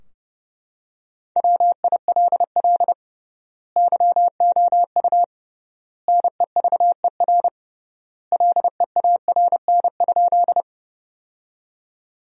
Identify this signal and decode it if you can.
Morse code, heard as a keyed tone, about 30 wpm, 700 Hz. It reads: WILL YOU NEVER LEARN?
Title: 3x Word-Spacing - Sentences from Top 300 Words - 30wpm